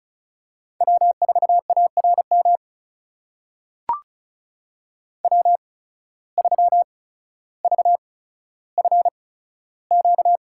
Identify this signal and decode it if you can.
Morse code, heard as a keyed tone, about 36 wpm, spaced as for 15 wpm, 700 Hz. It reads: W4ARM W3VFQ
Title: USA Call Signs - 5 Characters - 8x Character Spacing - 35wpm